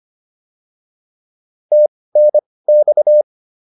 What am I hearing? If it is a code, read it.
Morse code, heard as a keyed tone, about 25 wpm, spaced as for 18 wpm, 600 Hz. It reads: TNX